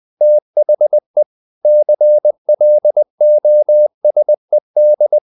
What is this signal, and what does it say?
Morse code, heard as a keyed tone, about 20 wpm, 600 Hz. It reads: THE CLOSED
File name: the old man and the sea part 2 20wpm